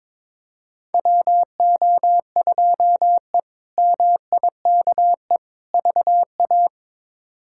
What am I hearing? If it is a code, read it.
Morse code, heard as a keyed tone, about 22 wpm, 700 Hz. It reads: WO2E MIKE VA
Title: K1USN SST QSO Practice - 22wpm